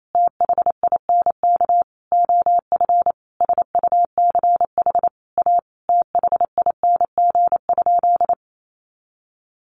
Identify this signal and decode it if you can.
Morse code, heard as a keyed tone, about 28 wpm, 700 Hz. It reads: THINK OF SUCH A THING?
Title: Sentences from Top 200 Words 28wpm